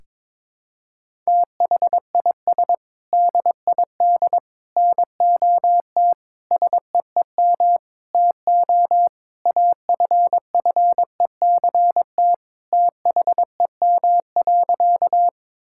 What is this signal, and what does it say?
Morse code, heard as a keyed tone, about 22 wpm, 700 Hz. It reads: THIS DID NOT SEEM TO AFFECT THEM.
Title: Sentences from Top 1000 Words - Review Part 6 - 22wpm